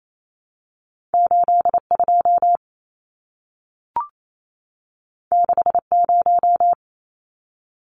Morse code, heard as a keyed tone, about 28 wpm, 700 Hz. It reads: 82 60
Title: Numbers - 2-Digit 28wpm